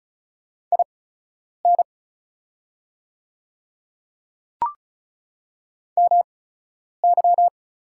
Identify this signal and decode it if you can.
Morse code, heard as a keyed tone, about 36 wpm, spaced as for 10 wpm, 690 Hz. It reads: IN MY
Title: 8x Character Spacing - Top 200 Words - 35wpm